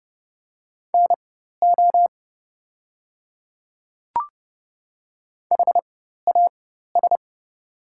Morse code, heard as a keyed tone, about 30 wpm, spaced as for 14 wpm, 690 Hz. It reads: NO HAS
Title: 4x Character Spacing - Top 100 Words - 30wpm